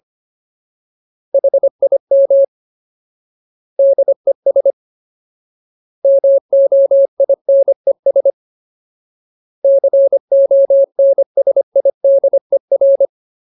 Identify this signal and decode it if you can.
Morse code, heard as a keyed tone, about 25 wpm, 550 Hz. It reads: HIM DES MOINES CONSIDER
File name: Ultimate Word List / 25WPM Ultimate